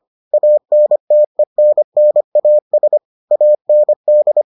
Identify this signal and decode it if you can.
Morse code, heard as a keyed tone, about 25 wpm, 600 Hz. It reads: ANTENNAS AND